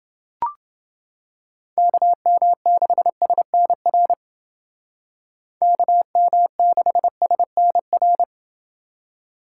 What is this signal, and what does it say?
Morse code, heard as a keyed tone, about 30 wpm, 700 Hz. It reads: KM6SNR KM6SNR